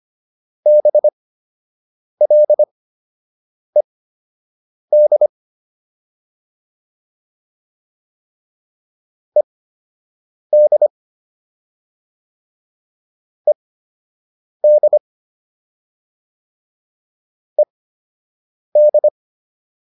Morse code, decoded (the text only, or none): BLED ED ED ED